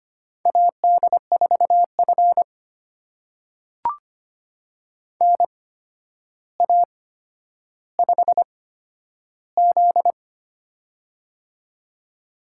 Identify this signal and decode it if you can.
Morse code, heard as a keyed tone, about 25 wpm, spaced as for 11 wpm, 700 Hz. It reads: AD4F NA5Z
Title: USA Call Signs - 4 Characters - 8x Character Spacing - 25wpm